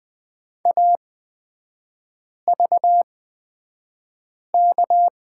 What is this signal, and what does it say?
Morse code, heard as a keyed tone, about 20 wpm, spaced as for 5 wpm, 700 Hz. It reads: AVK
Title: 224 Lesson 36e - BT - Three Characters 20wpm